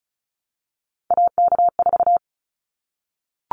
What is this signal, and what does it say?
Morse code, heard as a keyed tone, about 34 wpm, 700 Hz. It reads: AK4
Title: Callsign Prefixes for Top 25 DXCC - 35wpm